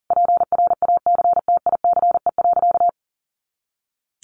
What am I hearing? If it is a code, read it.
Morse code, heard as a keyed tone, about 40 wpm, 700 Hz. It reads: PRACTICE.
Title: Easy Phrases 40wpm